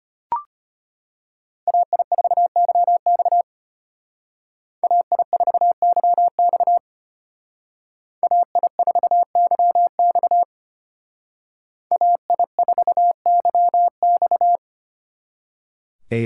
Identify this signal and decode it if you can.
Morse code, heard as a keyed tone, about 33 wpm, 700 Hz. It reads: AI4YX AI4YX AI4YX AI4YX